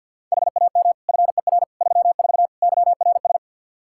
Morse code, heard as a keyed tone, about 49 wpm, 700 Hz. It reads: HAM FER 34 YRS
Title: Practice QSO - Exchange 3 - More Info - 50wpm